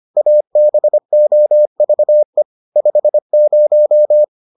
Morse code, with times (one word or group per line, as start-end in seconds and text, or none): 0.2-2.4 ABOVE
2.8-4.2 50